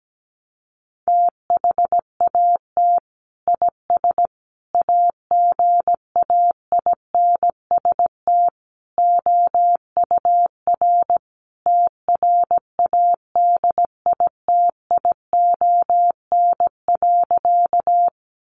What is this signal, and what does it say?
Morse code, heard as a keyed tone, about 17 wpm, 700 Hz. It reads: THAT IS AGAINST OUR TRADITION.
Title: Sentences from Top 1500 Words 17wpm